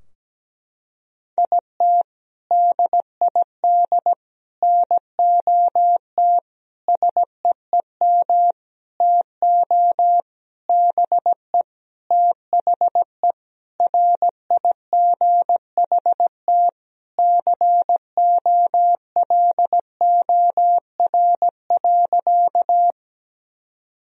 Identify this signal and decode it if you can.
Morse code, heard as a keyed tone, about 17 wpm, 700 Hz. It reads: IT DID NOT SEEM TO BE THE RIGHT COLOR.
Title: Sentences from Top 1000 Words - Review Part 2 - 17wpm